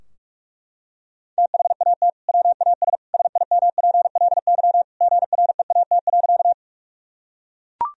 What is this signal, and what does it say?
Morse code, heard as a keyed tone, about 41 wpm, 700 Hz. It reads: THAT WAS SIMPLY GREAT.
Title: Sentences from Top 1000 Words - Review Part 4 - 45wpm